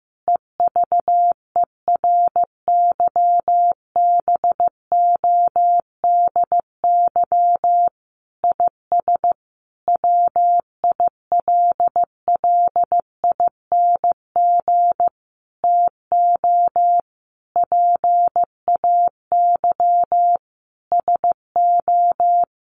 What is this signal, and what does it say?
Morse code, heard as a keyed tone, about 14 wpm, 700 Hz. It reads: EVERYBODY IS WILLING TO PAY SO
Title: Sentences from Top 1400 Words 15wpm